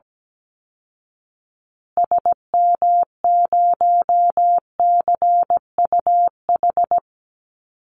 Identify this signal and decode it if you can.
Morse code, heard as a keyed tone, about 17 wpm, 700 Hz. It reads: SM0CUH
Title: Call Signs Repeated 2x - 17wpm